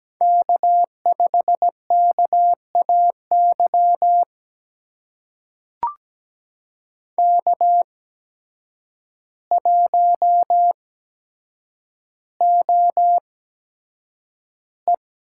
Morse code, heard as a keyed tone, about 17 wpm, spaced as for 8 wpm, 700 Hz. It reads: K5KAY K1OE
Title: USA Call Signs - 5 Characters - 8x Character Spacing - 17wpm